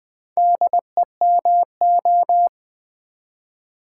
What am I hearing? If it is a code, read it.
Morse code, heard as a keyed tone, about 20 wpm, 700 Hz. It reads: DEMO
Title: Build A Word - 4 Letters At A Time - 10-Letter Words - 20wpm